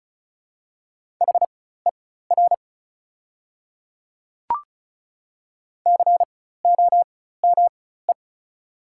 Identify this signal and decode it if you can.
Morse code, heard as a keyed tone, about 36 wpm, spaced as for 16 wpm, 690 Hz. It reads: HER COME
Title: 4x Character Spacing - Top 100 Words - 35wpm